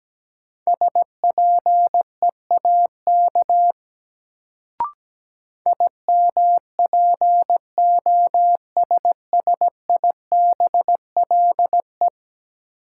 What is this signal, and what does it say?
Morse code, heard as a keyed tone, about 17 wpm, 700 Hz. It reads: SPEAK IMPOSSIBLE
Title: Sentences from Top 1100 Words - New Word Drill Alt - 17wpm